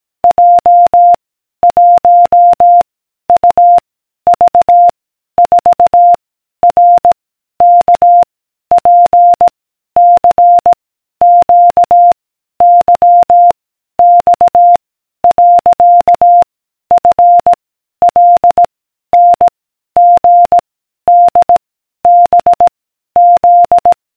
Morse code, heard as a keyed tone, about 18 wpm, 700 Hz. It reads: J 1 U V 4 R K P C Q Y X . F L N G D B Z